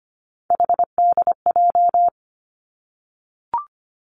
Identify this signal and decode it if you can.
Morse code, heard as a keyed tone, about 25 wpm, 700 Hz. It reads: HDJ
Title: Three Challenging Letter-Number Combinations 25wpm